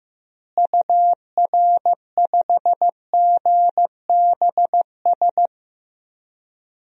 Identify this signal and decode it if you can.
Morse code, heard as a keyed tone, about 15 wpm, 700 Hz. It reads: UR5GBS
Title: DXCC-100 Call Signs - 6 Characters - Repeated 2x - 15wpm